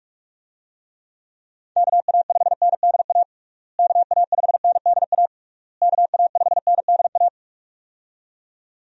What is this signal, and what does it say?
Morse code, heard as a keyed tone, about 46 wpm, 700 Hz. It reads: KA5NDA KA5NDA KA5NDA
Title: USA Call Signs - 6 Characters - Repeated 3x - 45wpm